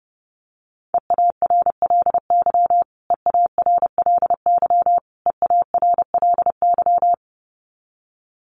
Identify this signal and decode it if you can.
Morse code, heard as a keyed tone, about 30 wpm, 700 Hz. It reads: EARLY EARLY EARLY
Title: Top 500 Words Repeated 30wpm